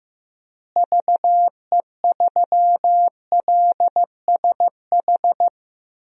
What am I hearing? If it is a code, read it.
Morse code, heard as a keyed tone, about 15 wpm, 700 Hz. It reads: VE3LSH
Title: DXCC-100 Call Signs - 6 Characters - Repeated 3x - 15wpm